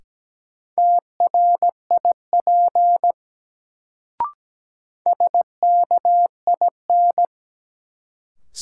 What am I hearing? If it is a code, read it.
Morse code, heard as a keyed tone, about 17 wpm, 700 Hz. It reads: TRIP SKIN
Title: Sentences from Top 1300 Words - New Word Drill Alt - 17wpm